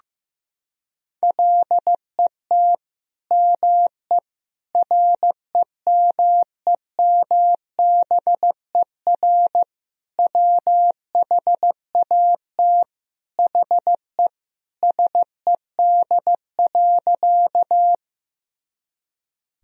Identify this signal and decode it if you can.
Morse code, heard as a keyed tone, about 15 wpm, 700 Hz. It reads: LET ME REMEMBER WHAT HE SED.
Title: Sentences from Top 400 Words and Abbreviations 15wpm